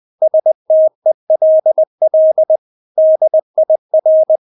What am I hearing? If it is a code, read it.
Morse code, heard as a keyed tone, about 20 wpm, 630 Hz. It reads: STELL DIR
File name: Witz1 20wpm